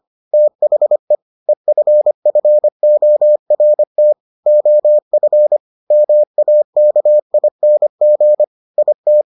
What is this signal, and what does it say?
Morse code, heard as a keyed tone, about 25 wpm, 600 Hz. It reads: THE EFFORT OF MAKING IT